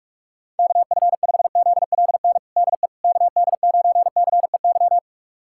Morse code, heard as a keyed tone, about 46 wpm, 700 Hz. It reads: KF5ZLN DE KD9CEY
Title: USA Callsign DE Callsign 45wpm